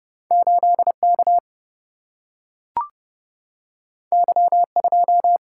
Morse code, heard as a keyed tone, about 31 wpm, 700 Hz. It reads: 8K Y2
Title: Single-Letter Single-Number 30wpm